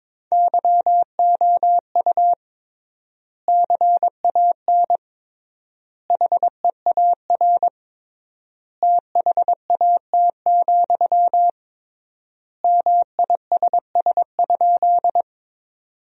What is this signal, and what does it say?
Morse code, heard as a keyed tone, about 22 wpm, 700 Hz. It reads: YOU CAN HEAR THAT, MISS?
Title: 3x Word-Spacing - Sentences from Top 500 Words - 22wpm